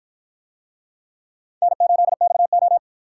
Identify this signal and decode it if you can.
Morse code, heard as a keyed tone, about 50 wpm, 700 Hz. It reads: N8XO